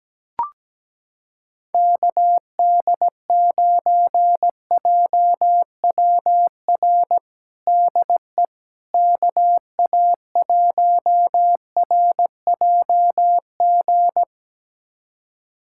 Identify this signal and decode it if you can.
Morse code, heard as a keyed tone, about 17 wpm, 700 Hz. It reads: KD9JWR DE KA1RJG
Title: USA Callsign DE Callsign 17wpm